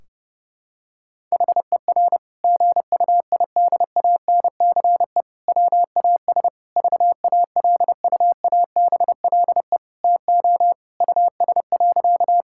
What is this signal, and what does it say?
Morse code, heard as a keyed tone, about 30 wpm, 700 Hz. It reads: HER GUIDANCE WAS VALUABLE TO US.
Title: Sentences from Top 1600 Words 30wpm